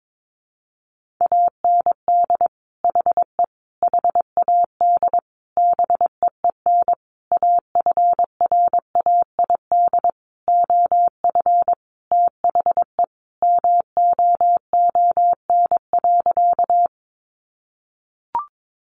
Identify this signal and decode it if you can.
Morse code, heard as a keyed tone, about 22 wpm, 700 Hz. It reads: AND HE HAD BEEN AFRAID OF THE MOON.
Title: Sentences from Top 1500 Words 22wpm